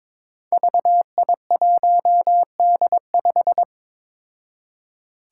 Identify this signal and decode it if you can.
Morse code, heard as a keyed tone, about 22 wpm, 700 Hz. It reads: VI1D5
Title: Five Challenging Letter-Number Combinations 22wpm